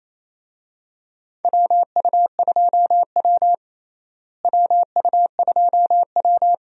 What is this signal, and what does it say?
Morse code, heard as a keyed tone, about 28 wpm, 700 Hz. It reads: WU2W WU2W
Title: USA Call Signs - 4 Characters - Repeated 3x - 28wpm